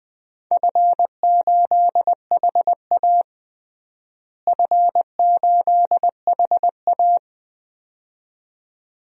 Morse code, heard as a keyed tone, about 21 wpm, 700 Hz. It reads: F8HA F8HA
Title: DXCC-100 Call Signs - 4 Characters - Repeated 2x - 20wpm